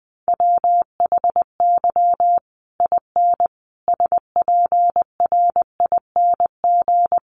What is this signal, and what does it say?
Morse code, heard as a keyed tone, about 19 wpm, 700 Hz. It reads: WHY IN SPRING
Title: Sentences from Top 1200 Words 20wpm